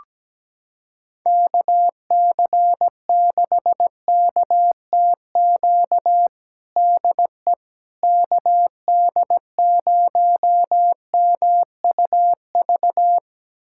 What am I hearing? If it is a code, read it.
Morse code, heard as a keyed tone, about 17 wpm, 700 Hz. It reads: KC6KTQ DE KD0MUV
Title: USA Callsign DE Callsign 17wpm